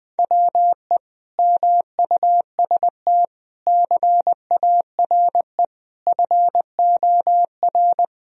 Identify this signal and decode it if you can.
Morse code, heard as a keyed tone, about 20 wpm, 700 Hz. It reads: WE MUST CARE FOR
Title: Sentences from Top 1000 Words - Review Part 1 - 20wpm